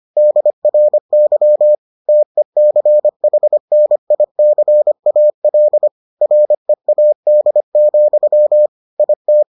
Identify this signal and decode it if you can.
Morse code, heard as a keyed tone, about 25 wpm, 600 Hz. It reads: DRY TECHNICAL READ, IT